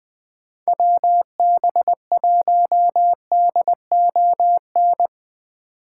Morse code, heard as a keyed tone, about 23 wpm, 700 Hz. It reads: WB1DON